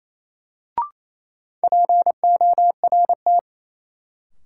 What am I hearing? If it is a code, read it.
Morse code, heard as a keyed tone, about 28 wpm, 700 Hz. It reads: PORT
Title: Sentences from Top 1000 Words - Review Part 1 - New Word Drill - 28wpm